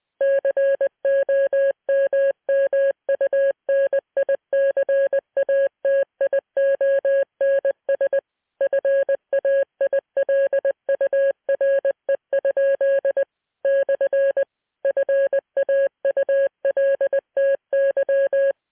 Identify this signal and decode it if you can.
Morse code, heard as a keyed tone, about 20 wpm, 550 Hz. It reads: COMMUNICATIONS FAILURE? / FAULTY